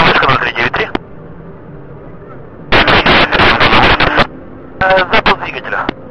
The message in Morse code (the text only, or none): none